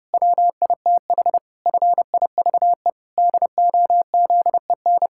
Morse code, heard as a keyed tone, about 30 wpm, 700 Hz. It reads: WITH FIVE DOZEN
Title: Pangram Challenge v2 - 30wpm